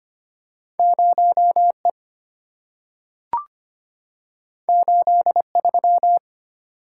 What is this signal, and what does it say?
Morse code, heard as a keyed tone, about 25 wpm, 700 Hz. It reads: 0E 83
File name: Two Letter-Number Combinations - 25wpm